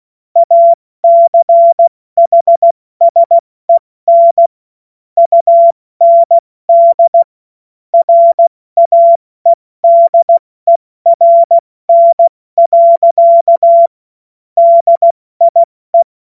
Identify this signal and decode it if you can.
Morse code, heard as a keyed tone, about 16 wpm, 680 Hz. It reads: ACHSEN UND RAEDERN. DIE
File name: Morsekurs / Witze / Witz 21. Juni 2022 70 F80